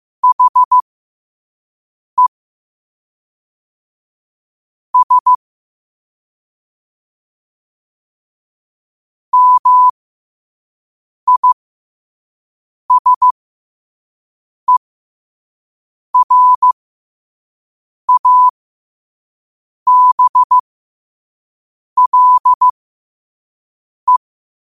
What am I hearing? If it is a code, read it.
Morse code, heard as a keyed tone, about 15 wpm, spaced as for 5 wpm, 1000 Hz. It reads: HE S MISERABLE
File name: qotdfiles / 5wpm